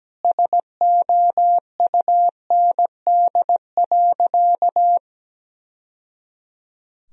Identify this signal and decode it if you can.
Morse code, heard as a keyed tone, about 17 wpm, 700 Hz. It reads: SOUND.
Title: Sentences from Top 1000 Words - Review Part 1 - 17wpm